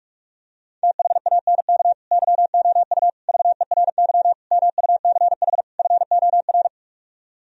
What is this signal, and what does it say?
Morse code, heard as a keyed tone, about 49 wpm, 700 Hz. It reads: THANK YOU VERY MUCH FOR